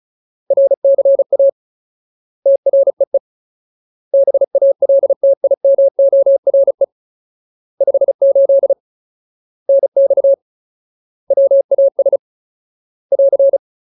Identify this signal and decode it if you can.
Morse code, heard as a keyed tone, about 34 wpm, 550 Hz. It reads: RCA TREE BALTIMORE 58 NX WAS <AR>
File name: Ultimate Word List / 35WPM Ultimate